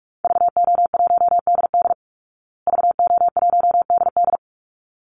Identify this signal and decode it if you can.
Morse code, heard as a keyed tone, about 44 wpm, 700 Hz. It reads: VO1DD VO1DD
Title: Call Signs Repeated 3x - 45wpm